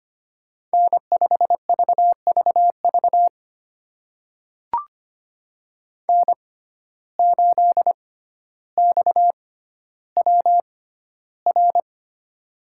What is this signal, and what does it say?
Morse code, heard as a keyed tone, about 25 wpm, spaced as for 13 wpm, 700 Hz. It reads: N5VVV N8XWR
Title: USA Call Signs - 5 Characters - 6x Character Spacing - 25wpm